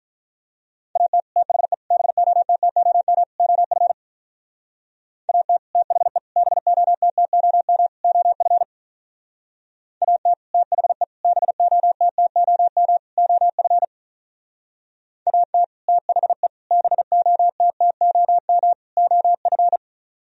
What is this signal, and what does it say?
Morse code, heard as a keyed tone, about 44 wpm, 700 Hz. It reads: AT THE BOTTOM OF AT THE BOTTOM OF AT THE BOTTOM OF AT THE BOTTOM OF